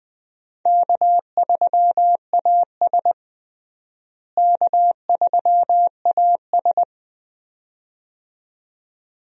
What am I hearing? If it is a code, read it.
Morse code, heard as a keyed tone, about 20 wpm, 700 Hz. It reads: K3AS K3AS
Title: USA Call Signs - 4 Characters - Repeated 2x - 20wpm